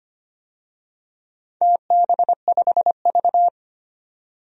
Morse code, heard as a keyed tone, about 25 wpm, 700 Hz. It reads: TB5V
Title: DXCC-100 Call Signs - 4 Characters - 25wpm